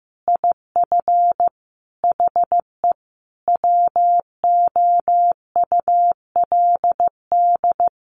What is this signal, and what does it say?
Morse code, heard as a keyed tone, about 15 wpm, 700 Hz. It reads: IF HE WOULD
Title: Sentences from Top 1300 Words 15wpm